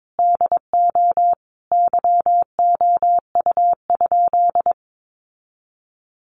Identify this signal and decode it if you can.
Morse code, heard as a keyed tone, about 22 wpm, 700 Hz. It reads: DO YOU?
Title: Sentences from Top 1200 Words 22wpm